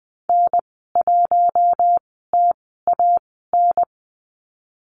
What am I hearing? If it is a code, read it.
Morse code, heard as a keyed tone, about 20 wpm, spaced as for 15 wpm, 700 Hz. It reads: N1TAN
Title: 042 Lesson 8g - Number 1 - Call Signs 20wpm